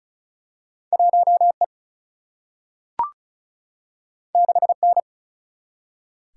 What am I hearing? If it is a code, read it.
Morse code, heard as a keyed tone, about 36 wpm, 700 Hz. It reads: 1 E 6 N
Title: Single-Letter Single-Number 35wpm